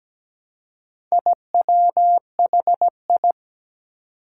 Morse code, heard as a keyed tone, about 17 wpm, 700 Hz. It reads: IWHI
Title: Four Challenging Letter-Number Combinations 17wpm